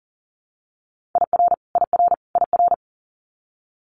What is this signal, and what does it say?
Morse code, heard as a keyed tone, about 38 wpm, 700 Hz. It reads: IR IR IR
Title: Multi-letter Phonemes Repeated 3x 40wpm